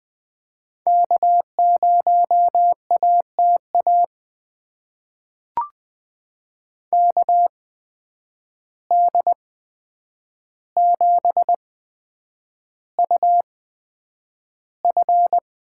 Morse code, heard as a keyed tone, about 20 wpm, spaced as for 9 wpm, 700 Hz. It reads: K0ATA KD7UF
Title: USA Call Signs - 8x Character Spacing - 20wpm